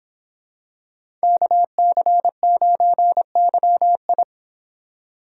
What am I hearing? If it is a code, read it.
Morse code, heard as a keyed tone, about 26 wpm, 700 Hz. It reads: KC9YI
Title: Speed Racing - USA Call Signs 22wpm